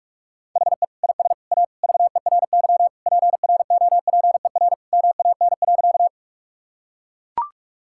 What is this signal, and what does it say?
Morse code, heard as a keyed tone, about 41 wpm, 700 Hz. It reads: HE IS A VERY PROPER MAN.
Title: Sentences from Top 1000 Words - Review Part 5 - 45wpm